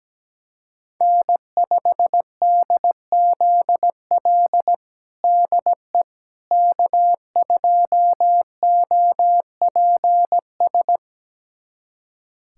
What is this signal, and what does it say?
Morse code, heard as a keyed tone, about 17 wpm, 700 Hz. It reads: N5DZL DE K2OPS
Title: USA Callsign DE Callsign - 5 Characters - 17wpm